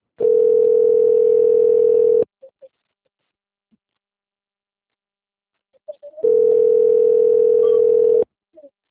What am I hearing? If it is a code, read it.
Ringback tone (North America), 440 + 480 Hz.